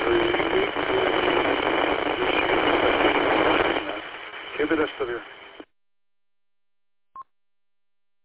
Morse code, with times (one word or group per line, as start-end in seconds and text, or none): none